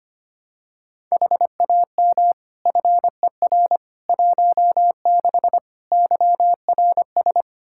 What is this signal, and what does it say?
Morse code, heard as a keyed tone, about 25 wpm, 700 Hz. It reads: HAM FER 16 YRS
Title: Practice QSO - Exchange 3 - More Info - 25wpm